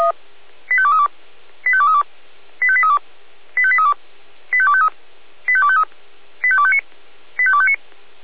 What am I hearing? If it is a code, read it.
Five-tone selective call (ZVEI), heard as quick runs of five tones, tones of about 70 ms. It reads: 85232 85232 85832 85832 85253 85253 85259 85259